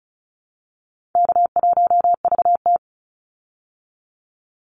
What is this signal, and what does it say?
Morse code, heard as a keyed tone, about 34 wpm, 700 Hz. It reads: K1VT
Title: Easy Call Signs Repeated 2x - 35wpm